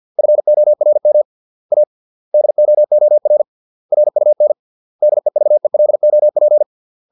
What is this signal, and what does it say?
Morse code, heard as a keyed tone, about 50 wpm, 600 Hz. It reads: FORM A DOOR RUN DEVELOP